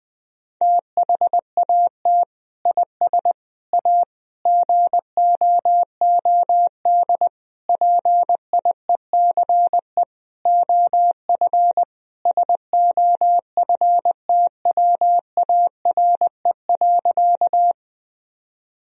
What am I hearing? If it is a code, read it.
Morse code, heard as a keyed tone, about 20 wpm, 700 Hz. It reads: THAT IS A GOOD PIECE OF SOFTWARE.
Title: Sentences from Top 1000 Words - Review Part 3 - 20wpm